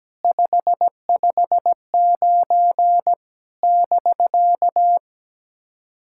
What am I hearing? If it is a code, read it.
Morse code, heard as a keyed tone, about 18 wpm, 700 Hz. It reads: 559 <BK>
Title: SOTA QSO Practice 17wpm